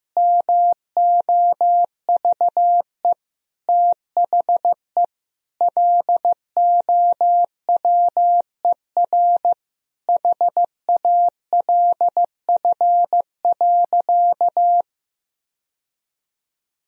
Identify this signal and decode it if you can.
Morse code, heard as a keyed tone, about 15 wpm, 700 Hz. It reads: MOVE THE LOWER HALF.
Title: Sentences from Top 1000 Words - Review Part 4 - 15wpm